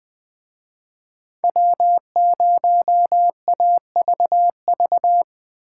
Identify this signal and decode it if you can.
Morse code, heard as a keyed tone, about 20 wpm, 700 Hz. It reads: W0AVV